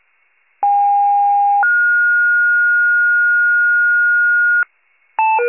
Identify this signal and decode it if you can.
Two-tone page, heard as a one-second tone, then a three-second tone, about 810 then 1430 Hz, with noise. It is heard from 0.6 to 4.6 s.